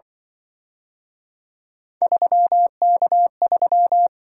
Morse code, heard as a keyed tone, about 25 wpm, 700 Hz. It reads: 3K3